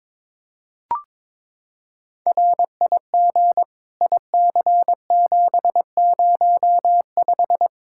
Morse code, heard as a keyed tone, about 22 wpm, 700 Hz. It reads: RIG IC705
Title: Practice QSO - Exchange 2 - Info - 22wpm